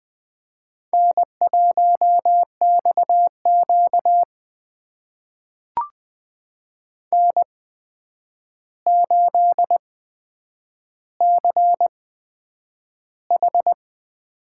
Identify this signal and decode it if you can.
Morse code, heard as a keyed tone, about 20 wpm, spaced as for 9 wpm, 700 Hz. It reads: N1XQ N8CH
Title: USA Call Signs - 4 Characters - 8x Character Spacing - 20wpm